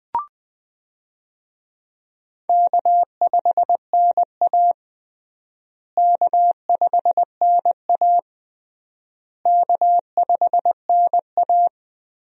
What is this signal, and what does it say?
Morse code, heard as a keyed tone, about 20 wpm, 700 Hz. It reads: K5NA K5NA K5NA
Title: USA Call Signs - 4 Characters - Repeated 3x - 20wpm